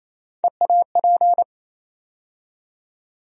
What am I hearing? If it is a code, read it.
Morse code, heard as a keyed tone, about 27 wpm, 700 Hz. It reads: EAP
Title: USA Call Sign Postfixes - 3 Letters 28wpm